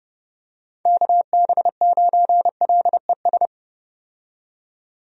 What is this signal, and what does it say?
Morse code, heard as a keyed tone, about 32 wpm, 700 Hz. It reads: KB9LES
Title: USA Call Signs - 6 Characters - 30wpm